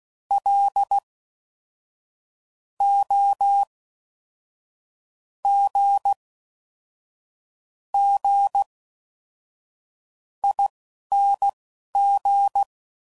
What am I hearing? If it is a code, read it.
Morse code, heard as a keyed tone, about 16 wpm, spaced as for 11 wpm, 800 Hz. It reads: L O G G ING
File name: Suffixes / Special INGs / Fast INGs